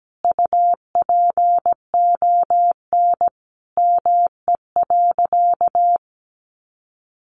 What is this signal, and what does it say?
Morse code, heard as a keyed tone, about 17 wpm, 700 Hz. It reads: UPON ME.
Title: Sentences from Top 1200 Words 17wpm